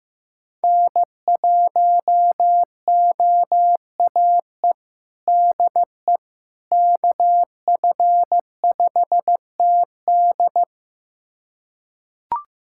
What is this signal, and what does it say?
Morse code, heard as a keyed tone, about 15 wpm, 700 Hz. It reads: N1OAE DE KF5TD
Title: USA Callsign DE Callsign - 5 Characters - 15wpm